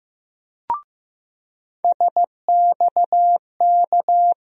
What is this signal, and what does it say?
Morse code, heard as a keyed tone, about 15 wpm, 700 Hz. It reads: SXK
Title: USA Call Sign Postfixes - 3 Letters 15wpm